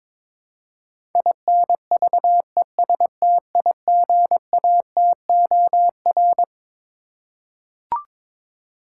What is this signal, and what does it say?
Morse code, heard as a keyed tone, about 22 wpm, 700 Hz. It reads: INVESTIGATOR
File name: Twelve Letter Words - 22wpm